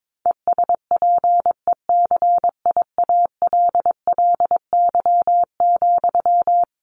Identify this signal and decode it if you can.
Morse code, heard as a keyed tone, about 22 wpm, 700 Hz. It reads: ESPECIALLY,
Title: Sentences from Top 700 Words 22wpm